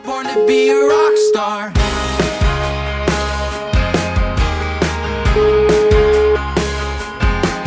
Ringback tone (Europe, 425 Hz), with noise.